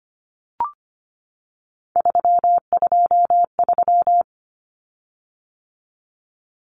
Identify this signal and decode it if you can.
Morse code, heard as a keyed tone, about 25 wpm, 700 Hz. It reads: 323
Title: Common Cut Numbers - 3-Digit 25wpm